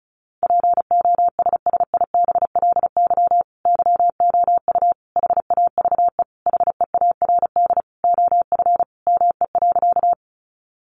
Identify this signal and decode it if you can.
Morse code, heard as a keyed tone, about 34 wpm, 700 Hz. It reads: POSSIBLY YOU HAVE HEARD OF ME.
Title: Sentences from Top 1300 Words 35wpm